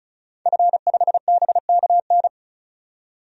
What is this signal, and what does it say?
Morse code, heard as a keyed tone, about 34 wpm, 700 Hz. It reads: F5BKN
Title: DXCC-100 Call Signs - 5 Characters - 35wpm